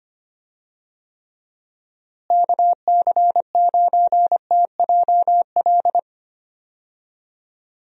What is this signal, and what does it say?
Morse code, heard as a keyed tone, about 25 wpm, 700 Hz. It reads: KC9TJL
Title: USA Call Signs - 6 Characters - Repeated 3x - 25wpm